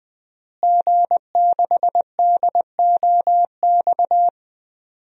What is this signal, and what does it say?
Morse code, heard as a keyed tone, about 20 wpm, 700 Hz. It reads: G6DOX
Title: DXCC-100 Call Signs - 5 Characters - Repeated 3x - 20wpm